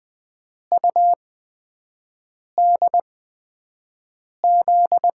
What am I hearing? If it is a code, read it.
Morse code, heard as a keyed tone, about 20 wpm, 700 Hz. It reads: U D Z
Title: USA Call Signs - 8x Character Spacing - 20wpm